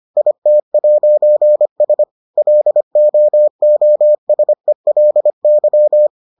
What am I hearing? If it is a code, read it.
Morse code, heard as a keyed tone, about 25 wpm, 600 Hz. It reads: IT'S LOOSELY